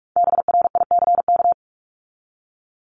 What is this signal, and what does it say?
Morse code, heard as a keyed tone, about 44 wpm, 700 Hz. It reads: BRICK
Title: Top 1751-2000 Words 45wpm